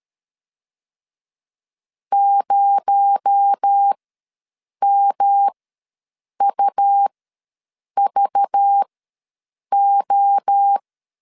Morse code, heard as a keyed tone, about 13 wpm, spaced as for 7 wpm, 800 Hz. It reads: 0MUVO
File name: LEZIONI / 9 2867 ws 12wpm 3431